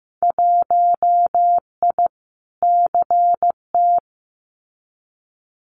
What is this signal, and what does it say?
Morse code, heard as a keyed tone, about 15 wpm, 700 Hz. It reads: 1I CT
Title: Winter Field Day Exchanges 15wpm